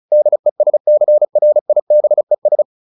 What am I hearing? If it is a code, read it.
Morse code, heard as a keyed tone, about 34 wpm, 600 Hz. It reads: DESCRIBES